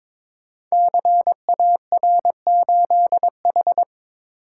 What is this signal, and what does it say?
Morse code, heard as a keyed tone, about 22 wpm, 700 Hz. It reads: CAR8H
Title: Code Groups - 5 Letter:Numbers - 22wpm